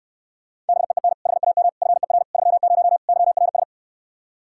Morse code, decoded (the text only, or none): BEEN HAM FER 30 YRS